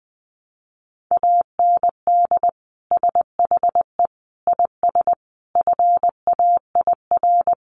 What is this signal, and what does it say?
Morse code, heard as a keyed tone, about 20 wpm, 700 Hz. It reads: AND SHE IS FAIR
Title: Sentences from Top 1300 Words 20wpm